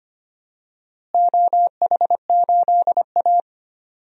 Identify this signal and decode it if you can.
Morse code, heard as a keyed tone, about 25 wpm, 700 Hz. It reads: OH8A
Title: DXCC-100 Call Signs - 4 Characters - Repeated 2x - 25wpm